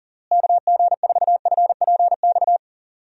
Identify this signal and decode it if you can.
Morse code, heard as a keyed tone, about 37 wpm, 700 Hz. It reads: KG4FPX
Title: USA Call Signs - 6 Characters - Repeated 3x - 40wpm